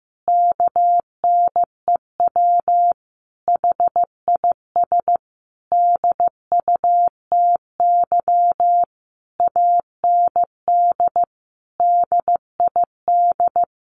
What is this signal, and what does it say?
Morse code, heard as a keyed tone, about 15 wpm, 700 Hz. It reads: KNEW HIS DUTY AND DID
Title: Sentences from Top 1500 Words 15wpm